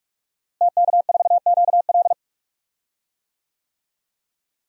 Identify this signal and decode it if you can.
Morse code, heard as a keyed tone, about 46 wpm, 700 Hz. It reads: TK4QL
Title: DXCC-100 Call Signs - 5 Characters - Repeated 3x - 45wpm